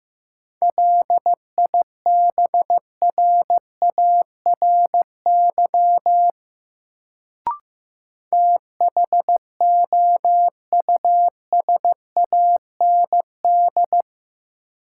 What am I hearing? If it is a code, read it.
Morse code, heard as a keyed tone, about 15 wpm, 700 Hz. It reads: LIBRARY THOUSAND
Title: Sentences from Top 600 Words - New Word Drill - 15wpm